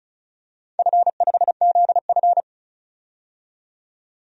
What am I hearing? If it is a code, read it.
Morse code, heard as a keyed tone, about 36 wpm, 700 Hz. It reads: F5ZF
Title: DXCC-100 Call Signs - 4 Characters - 35wpm